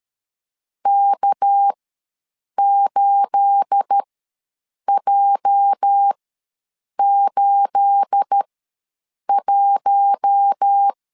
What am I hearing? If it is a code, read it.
Morse code, heard as a keyed tone, about 13 wpm, spaced as for 7 wpm, 800 Hz. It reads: K8J81
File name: LEZIONI / 9 28 ws 12wpm 3354